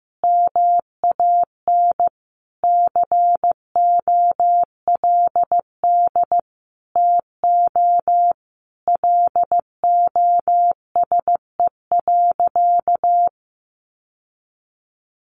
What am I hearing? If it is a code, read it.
Morse code, heard as a keyed tone, about 15 wpm, 700 Hz. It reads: MAN COLD TO LOSE.
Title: Sentences from Top 1100 Words 15wpm